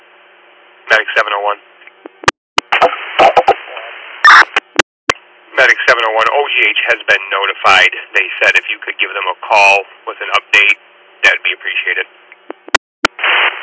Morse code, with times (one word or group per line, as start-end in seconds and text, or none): none